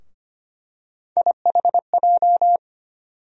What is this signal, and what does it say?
Morse code, heard as a keyed tone, about 25 wpm, 700 Hz. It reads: IHJ
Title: USA Call Sign Postfixes - 3 Letters 25wpm